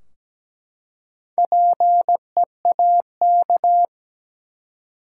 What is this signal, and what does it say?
Morse code, heard as a keyed tone, about 17 wpm, 700 Hz. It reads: PEAK
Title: Four Letter Words - 17wpm